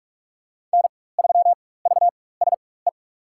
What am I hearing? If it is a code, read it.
Morse code, heard as a keyed tone, about 44 wpm, spaced as for 21 wpm, 700 Hz. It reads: N3VSE
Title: USA Call Signs - 5 Characters - 4x Character Spacing - 45wpm